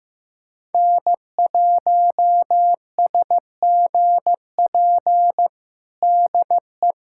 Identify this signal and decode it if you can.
Morse code, heard as a keyed tone, about 15 wpm, 700 Hz. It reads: N1SGP DE
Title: USA Callsign DE Callsign - 5 Characters - 15wpm